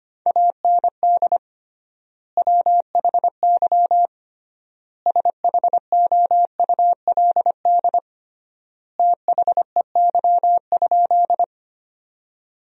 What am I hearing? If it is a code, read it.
Morse code, heard as a keyed tone, about 25 wpm, 700 Hz. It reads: AND WHY SHOULD THEY?
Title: 3x Word-Spacing - Sentences from Top 300 Words - 25wpm